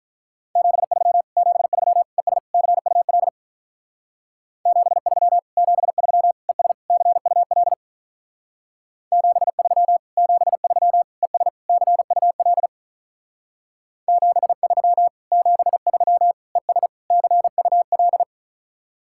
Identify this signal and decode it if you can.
Morse code, heard as a keyed tone, about 48 wpm, 700 Hz. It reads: 73 73 ES CUL 73 73 ES CUL 73 73 ES CUL 73 73 ES CUL